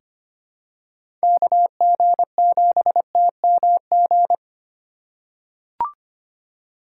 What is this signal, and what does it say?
Morse code, heard as a keyed tone, about 25 wpm, 700 Hz. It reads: KG7TMG
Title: USA Call Signs - 6 Characters - 3x Character Spacing - 25wpm